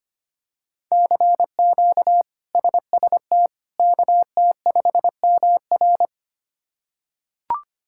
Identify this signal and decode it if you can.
Morse code, heard as a keyed tone, about 25 wpm, 700 Hz. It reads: CQ SST KT5MR
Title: K1USN SST QSO Practice - 25wpm